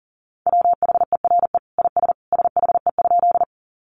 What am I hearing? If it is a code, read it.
Morse code, heard as a keyed tone, about 40 wpm, 700 Hz. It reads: WHERE IS SHE?
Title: Sentences from Top 200 Words 40wpm